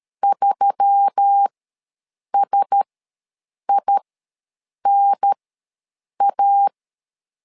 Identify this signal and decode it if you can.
Morse code, heard as a keyed tone, about 13 wpm, spaced as for 7 wpm, 800 Hz. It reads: 3SINA
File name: LEZIONI / 4 Q1S3 ws 12wpm 2038